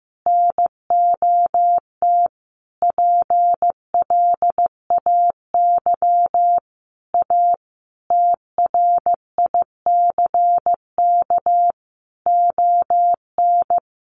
Morse code, heard as a keyed tone, about 15 wpm, 700 Hz. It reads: NOT PLAY A TRICK ON
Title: Sentences from Top 1500 Words 15wpm